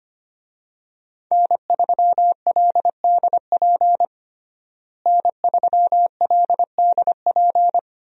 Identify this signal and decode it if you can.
Morse code, heard as a keyed tone, about 25 wpm, 700 Hz. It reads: N3LDP N3LDP
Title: USA Call Signs - 5 Characters - Repeated 3x - 25wpm